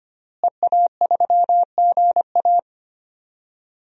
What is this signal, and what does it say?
Morse code, heard as a keyed tone, about 25 wpm, 700 Hz. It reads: EA3GA
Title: DXCC-100 Call Signs - 5 Characters - Repeated 3x - 25wpm